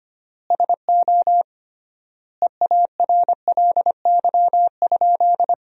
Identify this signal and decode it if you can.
Morse code, heard as a keyed tone, about 25 wpm, 700 Hz. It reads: SO EARLY?
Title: 3x Word-Spacing - Sentences from Top 400 Words - 25wpm